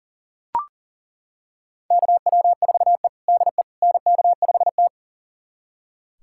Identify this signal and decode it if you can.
Morse code, heard as a keyed tone, about 40 wpm, 700 Hz. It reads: KW4E DE NK5T
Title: USA Callsign DE Callsign - 4 Characters - 40wpm